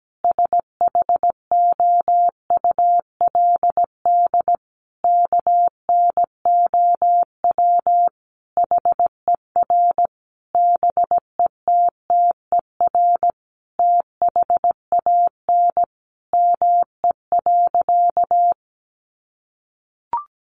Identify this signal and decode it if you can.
Morse code, heard as a keyed tone, about 17 wpm, 700 Hz. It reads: SHOULD KNOW HER BETTER THAN ME.
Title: Sentences from Top 400 Words 17wpm